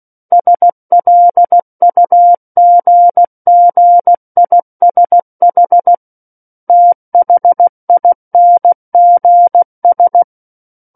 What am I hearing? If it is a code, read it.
Morse code, heard as a keyed tone, about 16 wpm, 700 Hz. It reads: SLUGGISH THINGS
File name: Book One / 07-Chapter 7 - 16 WPM